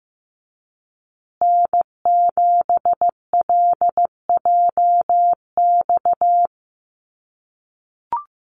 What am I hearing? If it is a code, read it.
Morse code, heard as a keyed tone, about 15 wpm, 700 Hz. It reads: N7LJX
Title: USA Call Signs Repeated 2x 15wpm